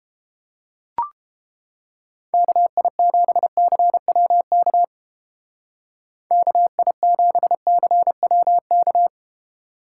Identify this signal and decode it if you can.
Morse code, heard as a keyed tone, about 29 wpm, 700 Hz. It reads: KI7CWK KI7CWK